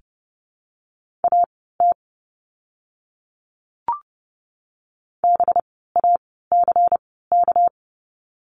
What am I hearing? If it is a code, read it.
Morse code, heard as a keyed tone, about 30 wpm, spaced as for 17 wpm, 700 Hz. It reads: AT BACK
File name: course / 30:10wpm / 230 Lesson 37f - Letter x - Words 30wpm